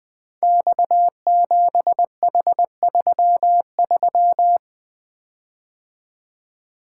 Morse code, heard as a keyed tone, about 20 wpm, 700 Hz. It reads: X7H33
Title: Code Groups - 5 Letter:Numbers - 20wpm